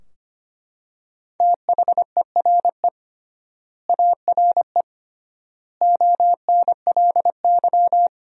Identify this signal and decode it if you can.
Morse code, heard as a keyed tone, about 25 wpm, 700 Hz. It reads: THERE ARE ONLY
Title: 3x Word-Spacing - Sentences from Top 200 Words - 25wpm